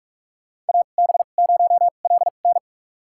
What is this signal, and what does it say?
Morse code, heard as a keyed tone, about 42 wpm, spaced as for 31 wpm, 700 Hz. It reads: AB0LN
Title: USA Call Signs - 5 Characters - 2x Character Spacing - 45wpm